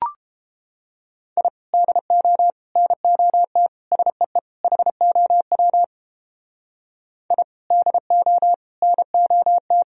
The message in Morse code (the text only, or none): I DO NOT SEE HOW I DO NOT